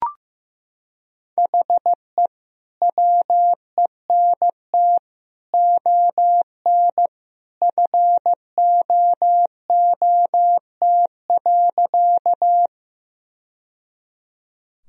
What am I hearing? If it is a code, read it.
Morse code, heard as a keyed tone, about 14 wpm, 700 Hz. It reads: HE WENT ON FOOT.
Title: Sentences from Top 1000 Words - Review Part 3 - 15wpm